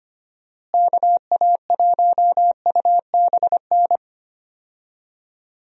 Morse code, heard as a keyed tone, about 25 wpm, 700 Hz. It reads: KA1UBN
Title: USA Call Signs - 6 Characters - Repeated 2x - 25wpm